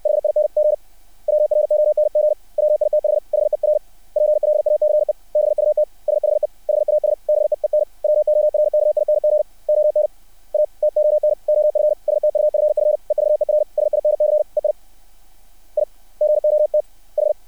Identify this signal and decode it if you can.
Morse code, heard as a keyed tone, about 60 wpm, 600 Hz. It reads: HING WRONG WITH HER PLACE BUT SHE HAS BEEN WORKING ON A TON OF STUFF EVER SINCE I I GOT H
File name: archived / CRSnet